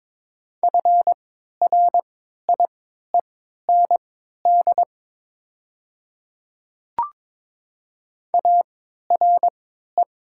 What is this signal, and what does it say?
Morse code, heard as a keyed tone, about 22 wpm, spaced as for 13 wpm, 700 Hz. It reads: FRIEND ARE
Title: 3x Character Spacing - Top 300 Words - 22wpm